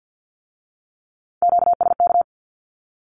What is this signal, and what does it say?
Morse code, heard as a keyed tone, about 50 wpm, 700 Hz. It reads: QSK
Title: Q-Codes with Spoken Meaning - 50wpm